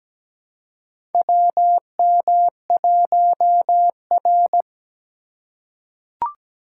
Morse code, heard as a keyed tone, about 17 wpm, 700 Hz. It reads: WM1R
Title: USA Call Signs - 4 Characters - 2x Character Spacing - 17wpm